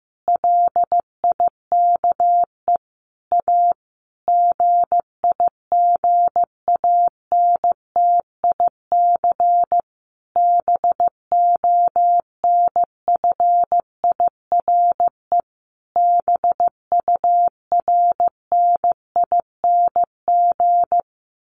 Morse code, heard as a keyed tone, about 15 wpm, 700 Hz. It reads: LIKE A GIGANTIC BONFIRE BURNING
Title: A Sound Of Thunder 15wpm